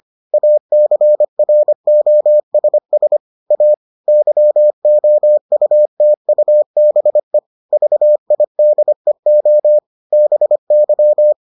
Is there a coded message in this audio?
Yes: Morse code from 0.3 to 11.3 s, heard as a keyed tone, about 25 wpm, 600 Hz. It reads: ACROSS A YOUTUBE VIDEO BY